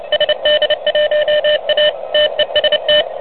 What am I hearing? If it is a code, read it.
Morse code, heard as a keyed tone, about 31 wpm, 590 Hz, with noise. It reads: SD1A TEST
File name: truSDX / MWC29.8